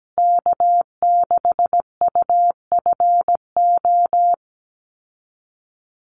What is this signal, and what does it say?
Morse code, heard as a keyed tone, about 17 wpm, 700 Hz. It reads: K6UFO
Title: Call Signs Repeated 3x - 17wpm